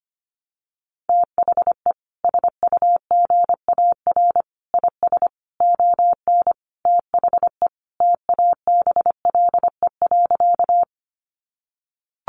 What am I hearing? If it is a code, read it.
Morse code, heard as a keyed tone, about 25 wpm, 700 Hz. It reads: THE SUGAR IS ON THE TABLE.
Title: Sentences from Top 1200 Words 25wpm